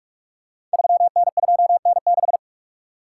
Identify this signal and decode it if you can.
Morse code, heard as a keyed tone, about 44 wpm, 690 Hz. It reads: 3N2N6
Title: Code Groups - 5 Letter:Numbers - 45wpm